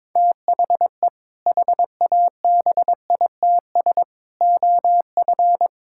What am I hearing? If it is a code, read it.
Morse code, heard as a keyed tone, about 22 wpm, 700 Hz. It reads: THE HABITS OF
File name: practice / 22wpm / Sentences with 2-Word Phrases - 22wpm